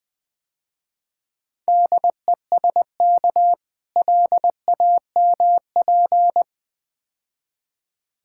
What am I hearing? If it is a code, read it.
Morse code, heard as a keyed tone, about 21 wpm, 700 Hz. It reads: DESK LAMP
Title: Difficult Words - Speed-Racing with Hints - 20wpm